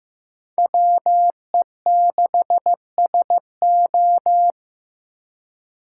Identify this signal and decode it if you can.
Morse code, heard as a keyed tone, about 18 wpm, 700 Hz. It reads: WE6SO
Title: Speed Racing - USA Call Signs 15wpm